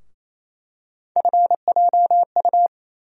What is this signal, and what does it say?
Morse code, heard as a keyed tone, about 27 wpm, 700 Hz. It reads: FJU
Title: USA Call Sign Postfixes - 3 Letters 28wpm